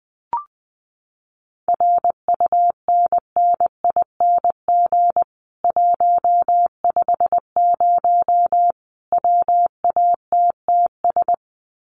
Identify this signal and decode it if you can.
Morse code, heard as a keyed tone, about 20 wpm, 700 Hz. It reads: RUNNING 150 WATTS
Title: Common QSO Phrases 20wpm